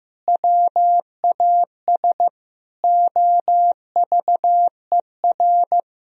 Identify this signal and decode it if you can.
Morse code, heard as a keyed tone, about 15 wpm, 700 Hz. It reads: WAS OVER
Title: Sentences from Top 1000 Words - Review Part 3 - 15wpm